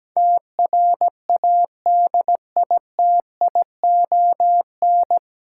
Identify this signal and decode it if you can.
Morse code, heard as a keyed tone, about 17 wpm, 700 Hz. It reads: TRADITION
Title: Sentences from Top 1500 Words - New Word Drill Alt - 17wpm